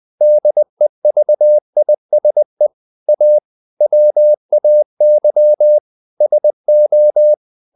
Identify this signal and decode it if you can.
Morse code, heard as a keyed tone, about 20 wpm, 600 Hz. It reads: DEVISE A WAY SO
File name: the old man and the sea part 2 20wpm